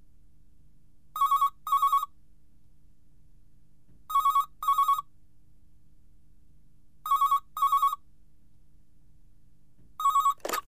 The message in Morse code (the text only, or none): none